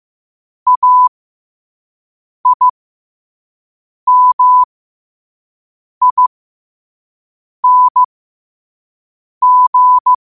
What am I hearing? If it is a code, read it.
Morse code, heard as a keyed tone, about 15 wpm, spaced as for 5 wpm, 1000 Hz. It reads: AIMING
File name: qotdfiles / 5wpm